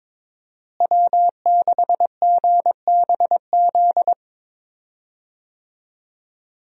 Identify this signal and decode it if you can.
Morse code, heard as a keyed tone, about 21 wpm, 700 Hz. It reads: W6GBZ
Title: USA Call Signs - 5 Characters - Repeated 2x - 22wpm